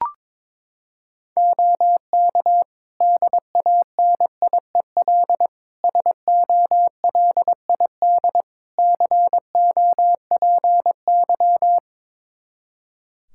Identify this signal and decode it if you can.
Morse code, heard as a keyed tone, about 22 wpm, 700 Hz. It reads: OK DANIEL SOLID COPY
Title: K1USN SST QSO Practice 22wpm